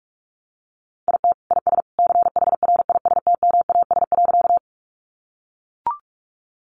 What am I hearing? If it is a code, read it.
Morse code, heard as a keyed tone, about 44 wpm, 700 Hz. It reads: IT IS CHRISTMAS.
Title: Easy Phrases 45wpm